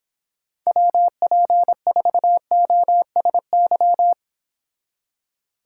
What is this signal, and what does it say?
Morse code, heard as a keyed tone, about 31 wpm, 700 Hz. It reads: WP4OSY